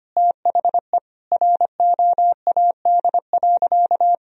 Morse code, heard as a keyed tone, about 25 wpm, 700 Hz. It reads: THE ROAD.
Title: Sentences from Top 1000 Words - Review Part 2 - 25wpm